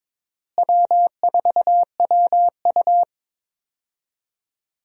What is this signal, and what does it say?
Morse code, heard as a keyed tone, about 22 wpm, 700 Hz. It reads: W4WU
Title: USA Call Signs - 4 Characters - Repeated 2x - 22wpm